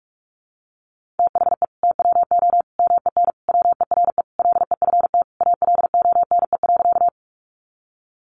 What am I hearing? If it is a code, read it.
Morse code, heard as a keyed tone, about 44 wpm, 700 Hz. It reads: THE TWO MEN WERE LEFT ALONE.
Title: Sentences from Top 1000 Words 45wpm